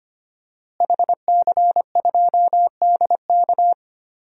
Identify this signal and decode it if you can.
Morse code, heard as a keyed tone, about 25 wpm, 700 Hz. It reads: HC2DK
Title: DXCC-100 Call Signs - 5 Characters - Repeated 3x - 25wpm